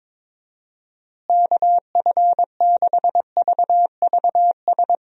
Morse code, heard as a keyed tone, about 22 wpm, 700 Hz. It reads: KF6VVS
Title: USA Call Signs - 6 Characters - Repeated 3x - 22wpm